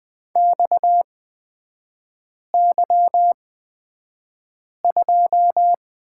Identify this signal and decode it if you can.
Morse code, heard as a keyed tone, about 20 wpm, spaced as for 5 wpm, 690 Hz. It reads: XY2
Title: 229 Lesson 37e - Letter x - Three Characters 20wpm